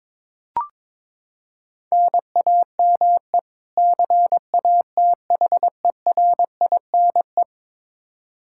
Morse code, heard as a keyed tone, about 22 wpm, 700 Hz. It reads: NAME CATHERINE
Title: Practice QSO - Exchange 1 - Beginning - 22wpm